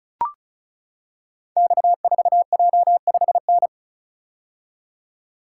Morse code, heard as a keyed tone, about 36 wpm, 700 Hz. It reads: X4J5N
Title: Code Groups - 5 Letter:Numbers - 35wpm